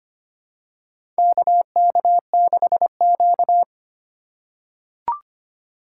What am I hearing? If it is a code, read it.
Morse code, heard as a keyed tone, about 25 wpm, 700 Hz. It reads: KK6Q
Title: USA Call Signs - 2x Character Spacing - 25wpm